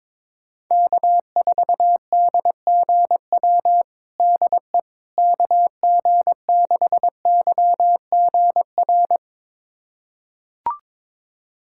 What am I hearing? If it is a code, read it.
Morse code, heard as a keyed tone, about 22 wpm, 700 Hz. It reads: K4DGW DE KG6YGR
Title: USA Callsign DE Callsign 22wpm